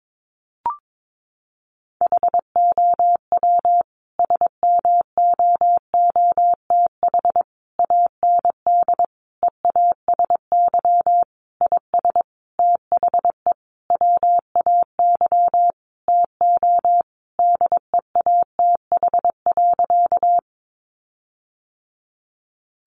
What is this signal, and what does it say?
Morse code, heard as a keyed tone, about 22 wpm, 700 Hz. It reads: HOW SMOOTH AND EASY IS THE WAY TO DEATH.
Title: Sentences from Top 1400 Words 22wpm